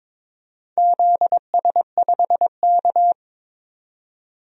Morse code, heard as a keyed tone, about 22 wpm, 700 Hz. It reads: ZS5K
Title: DXCC-100 Call Signs - 4 Characters - 22wpm